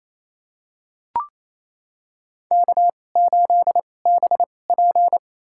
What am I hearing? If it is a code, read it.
Morse code, heard as a keyed tone, about 28 wpm, spaced as for 20 wpm, 700 Hz. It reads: K8BP
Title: USA Call Signs - 4 Characters - 2x Character Spacing - 28wpm